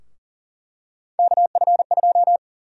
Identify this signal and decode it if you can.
Morse code, heard as a keyed tone, about 40 wpm, 700 Hz. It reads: KF2
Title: USA Call Sign Prefixes - 40wpm